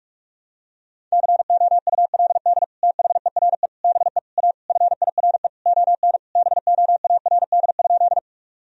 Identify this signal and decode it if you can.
Morse code, heard as a keyed tone, about 42 wpm, 700 Hz. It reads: COULD THERE BE A FIRE ON BOARD?
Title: Sentences from Top 1000 Words - Review Part 4 - 45wpm